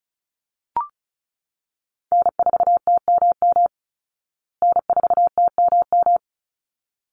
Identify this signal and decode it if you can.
Morse code, heard as a keyed tone, about 34 wpm, 700 Hz. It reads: N4TMM N4TMM
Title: Call Signs Repeated 3x - 35wpm